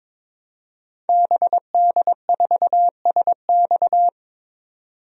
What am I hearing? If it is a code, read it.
Morse code, heard as a keyed tone, about 22 wpm, 700 Hz. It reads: BD4SX
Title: DXCC-100 Call Signs - 5 Characters - 22wpm